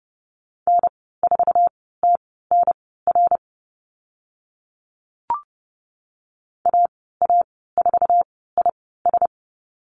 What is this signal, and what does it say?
Morse code, heard as a keyed tone, about 30 wpm, spaced as for 17 wpm, 690 Hz. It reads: N4TNR AA4IS
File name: Lesson 15g - Number 5 - Call Signs 30wpm